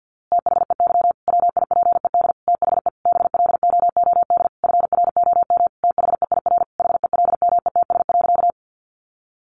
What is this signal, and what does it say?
Morse code, heard as a keyed tone, about 48 wpm, 700 Hz. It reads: THEY WIPED THE BLOOD FROM THEIR HELMETS.
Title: A Sound Of Thunder 50wpm